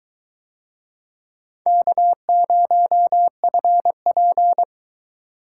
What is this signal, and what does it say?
Morse code, heard as a keyed tone, about 22 wpm, 700 Hz. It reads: K0FP